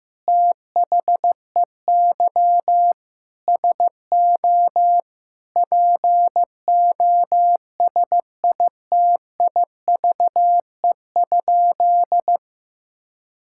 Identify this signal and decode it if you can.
Morse code, heard as a keyed tone, about 15 wpm, 700 Hz. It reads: THEY SO POSITIVE?
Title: Sentences from Top 1000 Words - Review Part 5 - 15wpm